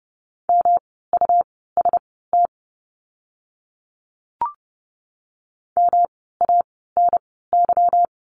Morse code, heard as a keyed tone, about 30 wpm, spaced as for 17 wpm, 700 Hz. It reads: MUST MANY
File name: course / 30:10wpm / 237 Lesson 38f - Letter q - Words 30wpm